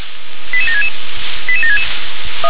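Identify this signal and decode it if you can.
Five-tone selective call (ZVEI), heard as quick runs of five tones, tones of about 70 ms. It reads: 88766 88766
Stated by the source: ZVEI five-tone call 88757 received on an SDR.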